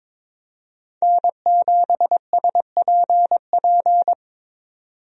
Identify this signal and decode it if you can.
Morse code, heard as a keyed tone, about 22 wpm, 700 Hz. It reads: N7SPP
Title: USA Call Signs - 5 Characters - 22wpm